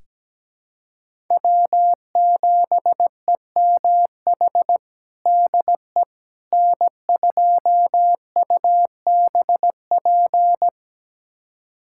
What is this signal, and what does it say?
Morse code, heard as a keyed tone, about 17 wpm, 700 Hz. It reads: W7EMH DE N2UBP
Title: USA Callsign DE Callsign - 5 Characters - 17wpm